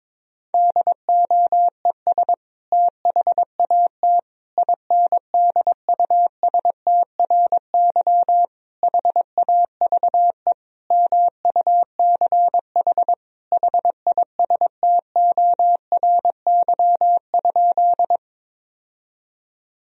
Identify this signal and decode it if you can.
Morse code, heard as a keyed tone, about 22 wpm, 700 Hz. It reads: DOES THAT INDUSTRY HAVE MUCH HISTORY?
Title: Sentences from Top 1000 Words - Review Part 4 - 22wpm